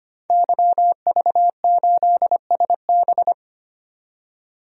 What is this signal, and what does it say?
Morse code, heard as a keyed tone, about 25 wpm, 700 Hz. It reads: YV8SB
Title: DXCC-100 Call Signs - 5 Characters - 25wpm